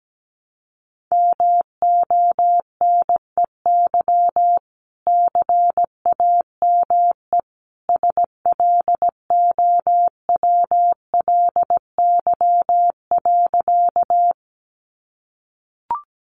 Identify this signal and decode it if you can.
Morse code, heard as a keyed tone, about 17 wpm, 700 Hz. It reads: MONEY CAME SLOWLY.
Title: Sentences from Top 1100 Words 17wpm